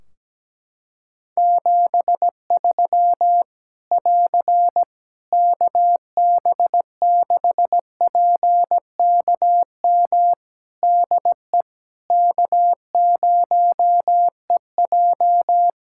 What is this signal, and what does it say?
Morse code, heard as a keyed tone, about 17 wpm, 700 Hz. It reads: 73 <AR> KB6PKM DE K0EJ
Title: Practice QSO - Exchange 4 - Ending - 17wpm